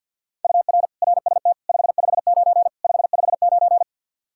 Morse code, heard as a keyed tone, about 51 wpm, 700 Hz. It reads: UR RST 559 559
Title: Practice QSO - Exchange 1 - Beginning - 50wpm